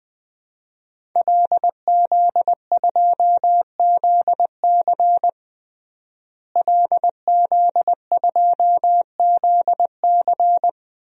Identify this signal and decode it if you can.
Morse code, heard as a keyed tone, about 20 wpm, 700 Hz. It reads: LZ2ZC LZ2ZC
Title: DXCC-100 Call Signs - 5 Characters - Repeated 2x - 20wpm